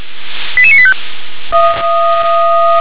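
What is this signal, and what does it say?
Five-tone selective call (ZVEI), heard as quick runs of five tones, tones of about 70 ms. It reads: 88985
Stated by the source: Five-tone ZVEI selective call 88972, received by SDR.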